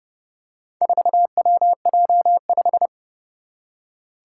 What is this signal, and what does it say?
Morse code, heard as a keyed tone, about 29 wpm, 700 Hz. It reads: 4WJ5
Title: Four Challenging Letter-Number Combinations 30wpm